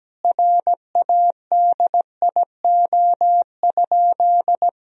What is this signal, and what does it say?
Morse code, heard as a keyed tone, about 17 wpm, 700 Hz. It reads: RADIO?
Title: Sentences from Top 1000 Words - Review Part 3 - 17wpm